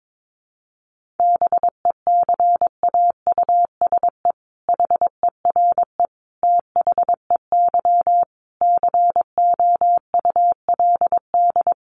Morse code, heard as a keyed tone, about 22 wpm, 700 Hz. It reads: BECAUSE HERE THEY COULD
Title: Sentences from Top 1500 Words 22wpm